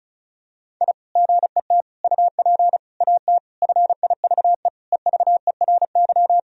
Morse code, heard as a keyed tone, about 36 wpm, 700 Hz. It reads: I GET UP AT FIVE EVERY
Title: Sentences from Top 1000 Words - Review Part 2 - 35wpm